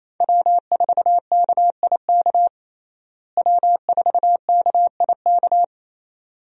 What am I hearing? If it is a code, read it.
Morse code, heard as a keyed tone, about 28 wpm, 700 Hz. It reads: W4KIK W4KIK
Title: USA Call Signs - 5 Characters - Repeated 2x - 28wpm